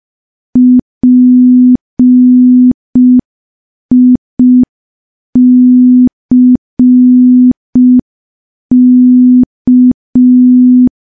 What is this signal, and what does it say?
Morse code, heard as a keyed tone, about 5 wpm, 250 Hz. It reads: PICK
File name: morse / morse-training-05WPM